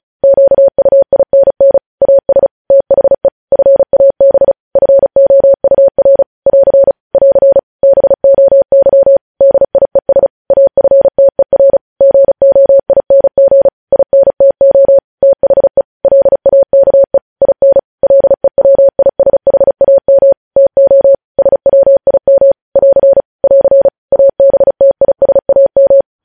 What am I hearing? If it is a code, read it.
Morse code, heard as a keyed tone, about 36 wpm, 550 Hz. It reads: QUINN AS THE FAB FOUR <AR> <AR> BOY DIES AFTER GOING INTO THE LAKE IN LEWISHAM TO SWIM <AR> <AR> ABTISAM
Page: News Headlines in Morse Code at 35 WPM